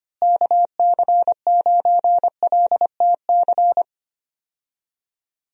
Morse code, heard as a keyed tone, about 25 wpm, 700 Hz. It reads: KC9LTC
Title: USA Call Signs - 6 Characters - Repeated 2x - 25wpm